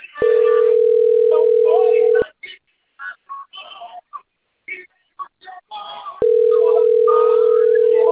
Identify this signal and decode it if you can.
Ringback tone (North America), 440 + 480 Hz, with speech.
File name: rg-1600-unknown- - - .24066